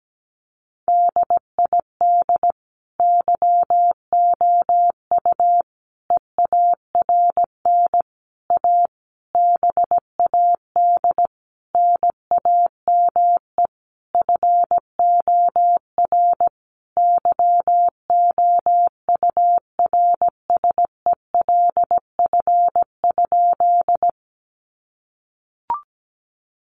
Morse code, heard as a keyed tone, about 17 wpm, 700 Hz. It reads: DID YOU EARN A BAD NAME FOR YOURSELF?
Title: Sentences from Top 1500 Words 17wpm